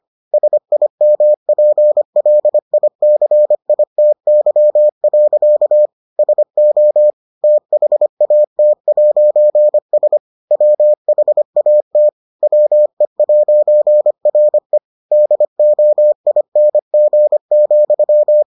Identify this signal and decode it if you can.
Morse code, heard as a keyed tone, about 25 wpm, 600 Hz. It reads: SIMPLICITY. SO THAT'S WHAT WE'RE DOING,